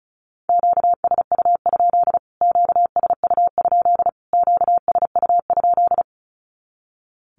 Morse code, heard as a keyed tone, about 34 wpm, 700 Hz. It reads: QSU? QSU? QSU?
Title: Q-Codes Repeated 3x 35wpm